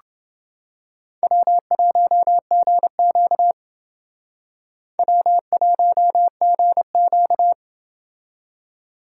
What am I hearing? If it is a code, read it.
Morse code, heard as a keyed tone, about 28 wpm, 700 Hz. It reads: W1GQ W1GQ